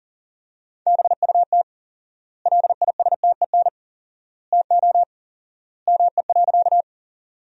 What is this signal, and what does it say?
Morse code, heard as a keyed tone, about 40 wpm, 700 Hz. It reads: BUT LISTEN TO ME.
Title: 4x Word-Spacing - Sentences from Top 400 Words - 40wpm